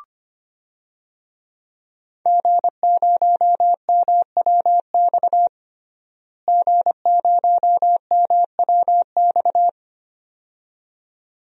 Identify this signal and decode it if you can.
Morse code, heard as a keyed tone, about 25 wpm, 700 Hz. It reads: G0MWX G0MWX
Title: DXCC-100 Call Signs - 5 Characters - Repeated 2x - 25wpm